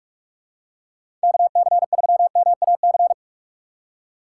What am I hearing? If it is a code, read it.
Morse code, heard as a keyed tone, about 43 wpm, 700 Hz. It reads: KC3MAC
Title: USA Call Signs - 6 Characters - 6x Character Spacing - 45wpm